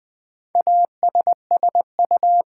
Morse code, heard as a keyed tone, about 20 wpm, 700 Hz. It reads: ASSU
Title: Build A Word - 4 Letters At A Time - 10-Letter Words - 20wpm